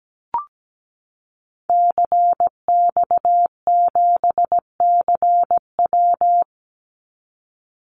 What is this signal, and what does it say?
Morse code, heard as a keyed tone, about 17 wpm, 700 Hz. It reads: CX7CW
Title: Call Signs Repeated 3x - 17wpm